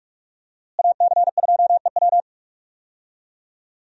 Morse code, heard as a keyed tone, about 44 wpm, 700 Hz. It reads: AC2EW
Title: USA Call Signs - 5 Characters - Repeated 3x - 45wpm